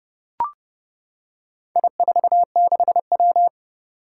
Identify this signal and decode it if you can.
Morse code, heard as a keyed tone, about 30 wpm, 700 Hz. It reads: I46W
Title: Four Challenging Letter-Number Combinations 30wpm